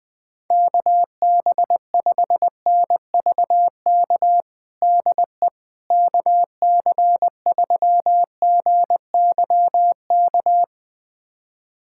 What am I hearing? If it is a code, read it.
Morse code, heard as a keyed tone, about 20 wpm, 690 Hz. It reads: KB5NVK DE KC3GYK
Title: USA Callsign DE Callsign 20wpm